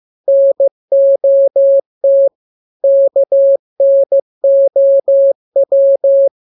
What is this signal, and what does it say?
Morse code, heard as a keyed tone, about 15 wpm, 550 Hz. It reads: NOT KNOW
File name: jokes / joke 26 15 0000